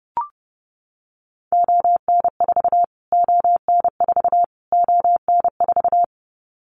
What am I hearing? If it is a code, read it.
Morse code, heard as a keyed tone, about 30 wpm, 700 Hz. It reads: ON4 ON4 ON4
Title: Callsign Prefixes for Top 25 DXCC - Repeated 3x - 30wpm